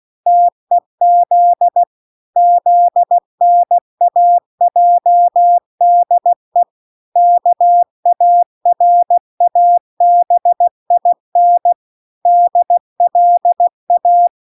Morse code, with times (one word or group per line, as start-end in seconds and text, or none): 0.3-1.8 TEZ
2.4-6.6 ZNAJDE
7.2-11.7 KARABIN
12.3-14.3 DLA